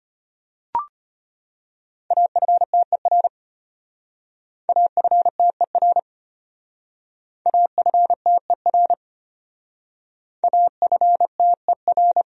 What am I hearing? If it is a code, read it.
Morse code, heard as a keyed tone, about 34 wpm, 700 Hz. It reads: AFTER AFTER AFTER AFTER